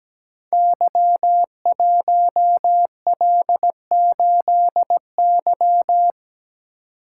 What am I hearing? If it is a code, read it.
Morse code, heard as a keyed tone, about 17 wpm, 690 Hz. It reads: Y1L8Y
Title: Code Groups - 5 Letter:Numbers - 17wpm